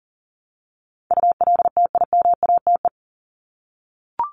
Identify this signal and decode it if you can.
Morse code, heard as a keyed tone, about 40 wpm, 700 Hz. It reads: ULTIMATE
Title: LY Suffix - Top 100 Words with Base Words 40wpm